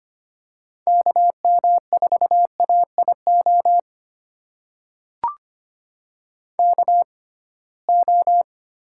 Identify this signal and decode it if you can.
Morse code, heard as a keyed tone, about 25 wpm, spaced as for 19 wpm, 700 Hz. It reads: KM4AIO KO
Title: USA Call Signs - 6x Character Spacing - 25wpm